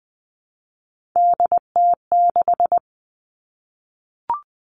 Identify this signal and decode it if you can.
Morse code, heard as a keyed tone, about 20 wpm, 700 Hz. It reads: DT6
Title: All Cut Numbers - 3-Digit 20wpm